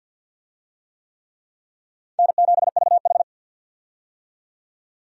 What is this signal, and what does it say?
Morse code, heard as a keyed tone, about 53 wpm, 700 Hz. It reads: N7VH
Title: USA Call Signs - 4 Characters - 50wpm